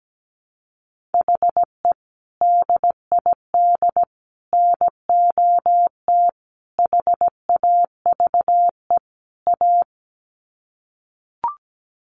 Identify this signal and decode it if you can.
Morse code, heard as a keyed tone, about 17 wpm, 700 Hz. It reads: HE DID NOT HAVE A
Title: Sets of 5 Words Encore 17wpm